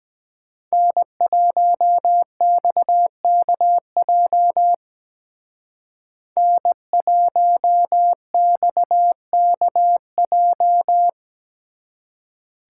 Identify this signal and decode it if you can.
Morse code, heard as a keyed tone, about 20 wpm, 700 Hz. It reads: N1XKJ N1XKJ
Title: Speed Racing - USA 1x3 Call Signs 17wpm